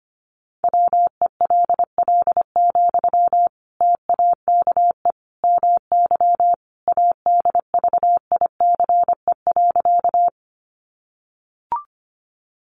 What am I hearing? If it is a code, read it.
Morse code, heard as a keyed tone, about 25 wpm, 700 Hz. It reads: WELL, TAKE MY ADVICE.
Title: Sentences from Top 1100 Words 25wpm